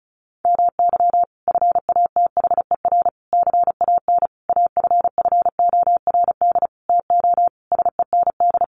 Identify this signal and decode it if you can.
Morse code, heard as a keyed tone, about 34 wpm, 700 Hz. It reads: MY FATHER CAN AFFORD TO SEND
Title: Sentences from Top 1500 Words 35wpm